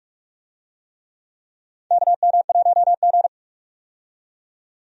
Morse code, heard as a keyed tone, about 46 wpm, 700 Hz. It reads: KM1G
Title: USA Call Signs - 4 Characters - 45wpm